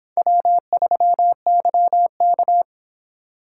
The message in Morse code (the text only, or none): W3YK